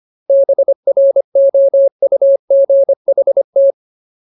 Morse code, heard as a keyed tone, about 25 wpm, 550 Hz. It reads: BROUGHT
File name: Ultimate Word List / 25WPM Ultimate